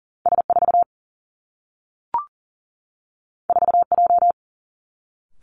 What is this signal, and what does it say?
Morse code, heard as a keyed tone, about 40 wpm, 700 Hz. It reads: S4 4J
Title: Two Challenging Letter-Number Combinations 40wpm